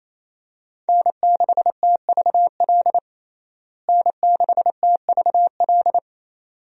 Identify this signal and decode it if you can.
Morse code, heard as a keyed tone, about 28 wpm, 700 Hz. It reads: N6TVL N6TVL
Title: USA Call Signs - 5 Characters - Repeated 2x - 28wpm